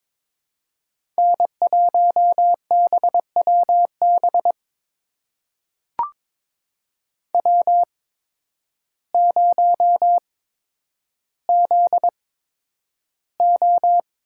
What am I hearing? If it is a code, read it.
Morse code, heard as a keyed tone, about 22 wpm, spaced as for 10 wpm, 700 Hz. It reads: N1BWB W0ZO
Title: USA Call Signs - 5 Characters - 8x Character Spacing - 22wpm